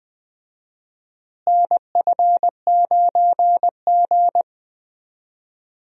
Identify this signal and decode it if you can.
Morse code, heard as a keyed tone, about 20 wpm, 700 Hz. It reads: NF9G
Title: USA Call Signs - 4 Characters - Repeated 3x - 20wpm